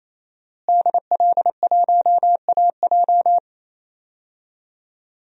Morse code, heard as a keyed tone, about 27 wpm, 700 Hz. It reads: DL1AJ
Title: DXCC-100 Call Signs - 5 Characters - Repeated 3x - 28wpm